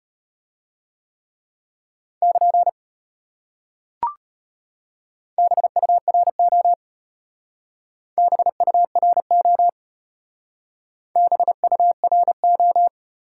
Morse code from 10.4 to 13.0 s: BURO

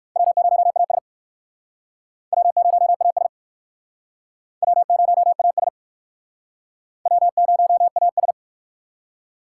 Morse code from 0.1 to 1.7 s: W0AS

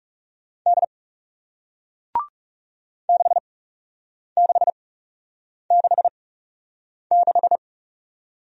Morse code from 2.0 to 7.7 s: 6666

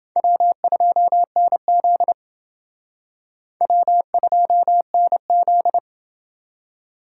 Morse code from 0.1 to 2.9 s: W2NZ

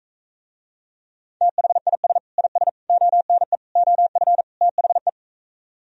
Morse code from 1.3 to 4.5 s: THIS IS ONE OF

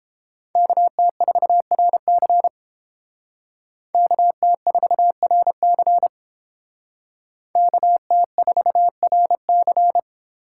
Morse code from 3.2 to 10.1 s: KT4RC KT4RC

